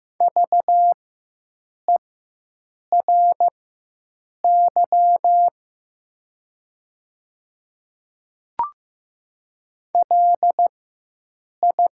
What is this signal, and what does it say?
Morse code, heard as a keyed tone, about 15 wpm, spaced as for 7 wpm, 700 Hz. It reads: VERY LI
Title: 4x Character Spacing - Top 300 Words - 15wpm